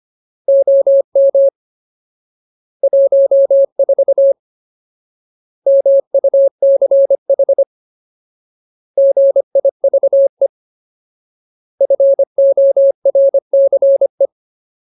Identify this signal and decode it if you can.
Morse code, heard as a keyed tone, about 25 wpm, 550 Hz. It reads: OM 14 MUCH GIVE FORCE
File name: Ultimate Word List / 25WPM Ultimate